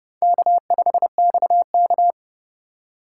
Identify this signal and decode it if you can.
Morse code, heard as a keyed tone, about 26 wpm, 700 Hz. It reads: K5XK